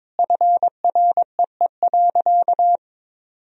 Morse code, heard as a keyed tone, about 22 wpm, 700 Hz. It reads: FREE.
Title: Sentences from Top 1000 Words - Review Part 1 - 22wpm